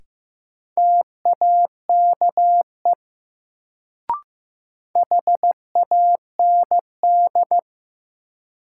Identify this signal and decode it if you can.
Morse code, heard as a keyed tone, about 15 wpm, 700 Hz. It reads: TAKE HAND
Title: Sentences from Top 200 Words - New Word Drill - 15wpm